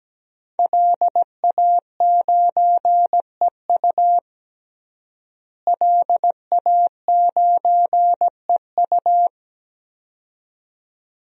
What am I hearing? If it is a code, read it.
Morse code, heard as a keyed tone, about 17 wpm, 700 Hz. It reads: LA9EU LA9EU
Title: DXCC-100 Call Signs - 5 Characters - Repeated 2x - 17wpm